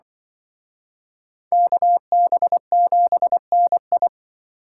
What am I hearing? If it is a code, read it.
Morse code, heard as a keyed tone, about 24 wpm, 700 Hz. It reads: KB7NI
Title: Speed Racing - USA 2x2 Call Signs 20wpm